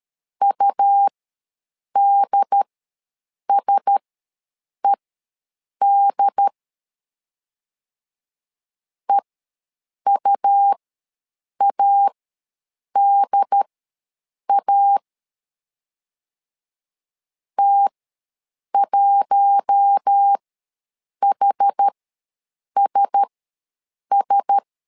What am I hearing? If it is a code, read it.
Morse code, heard as a keyed tone, about 13 wpm, spaced as for 7 wpm, 800 Hz. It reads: UDSED EUADA T1HSS
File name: LEZIONI / 4 Q1S ws 12wpm 1827